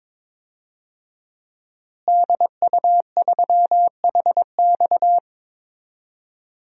Morse code, heard as a keyed tone, about 22 wpm, 700 Hz. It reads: DU3HX
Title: DXCC-100 Call Signs - 5 Characters - Repeated 2x - 22wpm